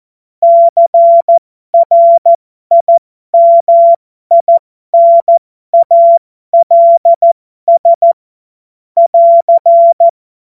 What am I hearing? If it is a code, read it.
Morse code, heard as a keyed tone, about 14 wpm, spaced as for 12 wpm, 680 Hz. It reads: CRIMINALS <AR>
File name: Morsekurs / Witze